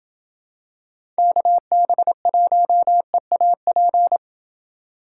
Morse code, heard as a keyed tone, about 28 wpm, 700 Hz. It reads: KB1EAP